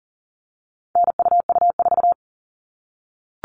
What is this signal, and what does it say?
Morse code, heard as a keyed tone, about 40 wpm, 700 Hz. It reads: NUU4
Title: All Cut Numbers - 4-Digit 40wpm